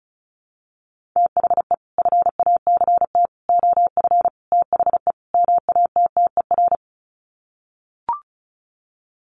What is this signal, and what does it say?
Morse code, heard as a keyed tone, about 34 wpm, 700 Hz. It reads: THE FACT OF THE MATTER
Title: Sets of 5 Words 35wpm